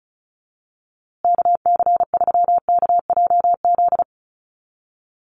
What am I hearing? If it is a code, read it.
Morse code, heard as a keyed tone, about 34 wpm, 700 Hz. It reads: KC3KJZ
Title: USA Call Signs Repeated 2x 35wpm